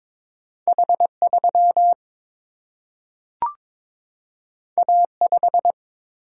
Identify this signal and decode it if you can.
Morse code, heard as a keyed tone, about 22 wpm, 700 Hz. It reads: H 3 A 5
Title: Single-Letter Single-Number 22wpm